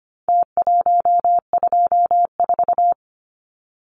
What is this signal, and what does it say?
Morse code, heard as a keyed tone, about 25 wpm, 700 Hz. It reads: T124
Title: Numbers - Contest Style - 4-Digit 25wpm